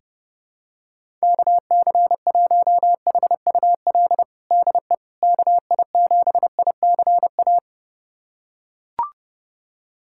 Morse code, heard as a keyed tone, about 30 wpm, 700 Hz. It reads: KC1HUL DE KI7ICA
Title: USA Callsign DE Callsign - 6 Characters - 30wpm